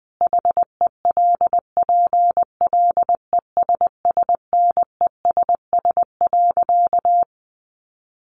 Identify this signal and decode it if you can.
Morse code, heard as a keyed tone, about 20 wpm, 700 Hz. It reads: HELPLESSNESS.
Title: A Sound Of Thunder 20wpm